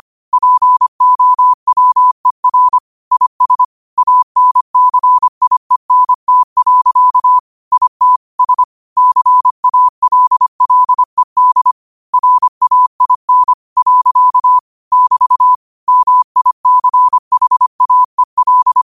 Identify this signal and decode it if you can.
Morse code, heard as a keyed tone, about 25 wpm, 1000 Hz. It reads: POWER IS ANCIENT. IT S CALLED RAIN. <BT> MICHAEL
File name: qotdfiles / 25wpm